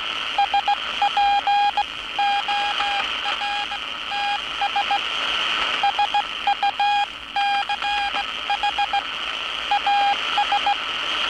Morse code, heard as a keyed tone, about 17 wpm, spaced as for 15 wpm, 810 Hz, with radio noise. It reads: SPORTS SUCH AS
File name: training / NAVPERS 11049to11060 / NAVPERS 11059 RA RB